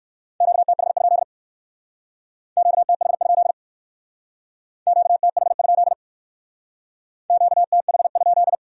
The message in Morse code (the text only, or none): QTH? QTH? QTH? QTH?